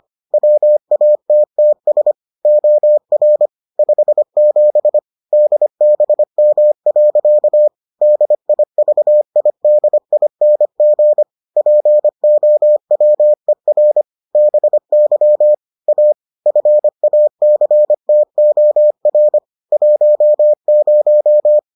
Morse code, heard as a keyed tone, about 25 wpm, 600 Hz. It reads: WATTS OR 57 DBM. DIVIDING POWER BY A FACTOR 10